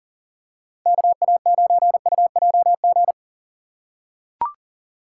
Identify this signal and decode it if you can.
Morse code, heard as a keyed tone, about 40 wpm, 690 Hz. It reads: KA9UJG
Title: USA Call Signs - 6 Characters - 2x Character Spacing - 40wpm